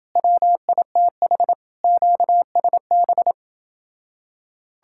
Morse code, heard as a keyed tone, about 31 wpm, 700 Hz. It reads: WITH QSB